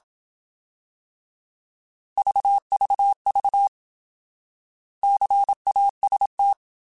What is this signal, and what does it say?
Morse code, heard as a keyed tone, about 27 wpm, 800 Hz. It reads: VVV CAST
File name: Alphabetical